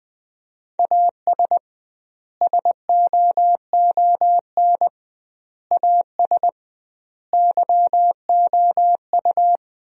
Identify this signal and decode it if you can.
Morse code, heard as a keyed tone, about 21 wpm, 700 Hz. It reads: AS SOON AS YOU
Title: 2x Word-Spacing - Sentences from Top 400 Words - 20wpm